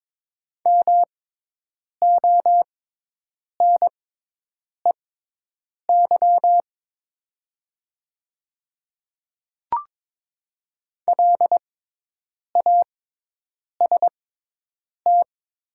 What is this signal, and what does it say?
Morse code, heard as a keyed tone, about 22 wpm, spaced as for 8 wpm, 700 Hz. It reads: MONEY LAST
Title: 6x Character Spacing - Top 500 Words - 22wpm